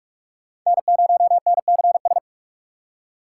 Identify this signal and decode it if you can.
Morse code, heard as a keyed tone, about 41 wpm, 700 Hz. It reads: N0NCS
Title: USA Call Signs - 5 Characters - 4x Character Spacing - 45wpm